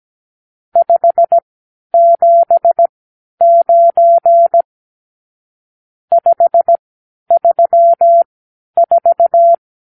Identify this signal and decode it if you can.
Morse code, heard as a keyed tone, about 17 wpm, spaced as for 10 wpm, 680 Hz. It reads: 579 534
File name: arrl / Review (1-0)